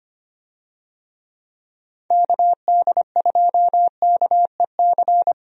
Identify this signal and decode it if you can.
Morse code, heard as a keyed tone, about 25 wpm, 700 Hz. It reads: KD2KEC
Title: USA Call Signs - 6 Characters - Repeated 2x - 25wpm